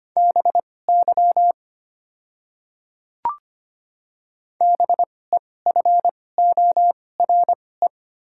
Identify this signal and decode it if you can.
Morse code, heard as a keyed tone, about 25 wpm, spaced as for 18 wpm, 700 Hz. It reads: BY BEFORE
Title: 2x Character Spacing - Top 300 Words - 25wpm